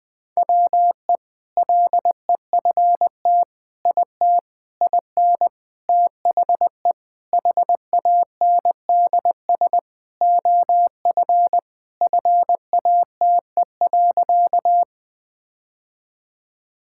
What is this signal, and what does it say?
Morse code, heard as a keyed tone, about 21 wpm, 700 Hz. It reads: WE LEFT IT IN THE HANDS OF FATE.
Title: Sentences with 4-Word Phrases - 20wpm